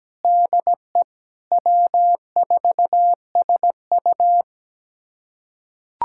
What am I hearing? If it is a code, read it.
Morse code, heard as a keyed tone, about 17 wpm, 700 Hz. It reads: DE W4SU
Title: USA Callsign DE Callsign - 4 Characters - 17wpm